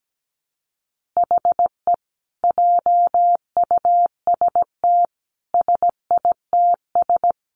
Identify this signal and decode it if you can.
Morse code, heard as a keyed tone, about 17 wpm, 700 Hz. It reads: HE JUST SITS
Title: Additional Easy Phrases 17wpm